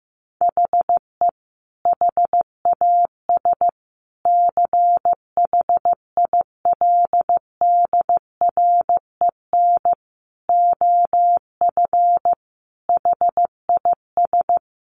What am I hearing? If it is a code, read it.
Morse code, heard as a keyed tone, about 15 wpm, 700 Hz. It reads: HE HAS CHILDREN OF HIS
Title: Sentences from Top 300 Words 15wpm